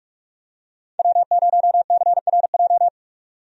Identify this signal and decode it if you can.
Morse code, heard as a keyed tone, about 43 wpm, 700 Hz. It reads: W0CRJ
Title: USA Call Signs - 5 Characters - 45wpm